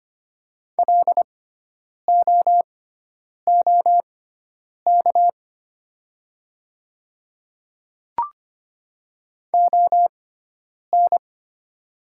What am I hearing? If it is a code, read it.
Morse code, heard as a keyed tone, about 25 wpm, spaced as for 9 wpm, 700 Hz. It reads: LOOK ON
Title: 6x Character Spacing - Top 100 Words - 25wpm